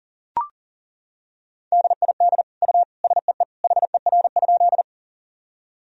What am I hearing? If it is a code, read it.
Morse code, heard as a keyed tone, about 40 wpm, 700 Hz. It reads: DID U SEE HER?
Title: Sentences from Top 100 Words and Abbreviations 40wpm